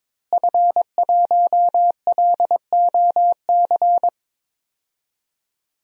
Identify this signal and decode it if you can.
Morse code, heard as a keyed tone, about 22 wpm, 700 Hz. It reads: F1LOC
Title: DXCC-100 Call Signs - 5 Characters - Repeated 3x - 22wpm